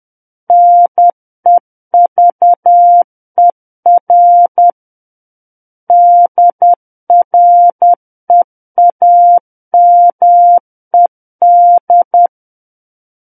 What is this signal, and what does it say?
Morse code, heard as a keyed tone, about 10 wpm, 700 Hz. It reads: NEVER DREAMED
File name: Book One / 01-Chapter 1 - 10 WPM